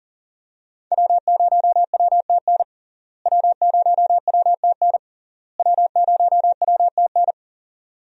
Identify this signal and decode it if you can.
Morse code, heard as a keyed tone, about 40 wpm, 700 Hz. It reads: W0WTN W0WTN W0WTN
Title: USA Call Signs - 5 Characters - Repeated 3x - 40wpm